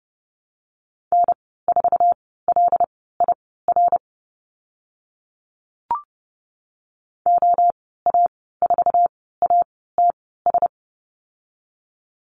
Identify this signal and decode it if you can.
Morse code, heard as a keyed tone, about 30 wpm, spaced as for 17 wpm, 690 Hz. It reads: N4LIR OA4ATS
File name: Lesson 15g - Number 5 - Call Signs 30wpm